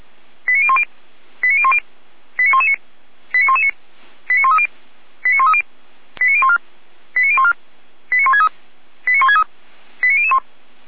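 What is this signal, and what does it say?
Five-tone selective call (ZVEI), heard as quick runs of five tones, tones of about 70 ms. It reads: 89010 89010 89119 89119 89130 89130 89014 89014 89163 89163 89001
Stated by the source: ZVEI five-tone selective call received on SDR.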